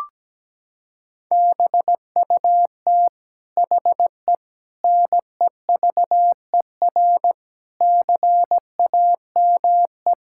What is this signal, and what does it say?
Morse code, heard as a keyed tone, about 17 wpm, 700 Hz. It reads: BUT HE NEVER CAME
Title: Sentences from Top 1600 Words 17wpm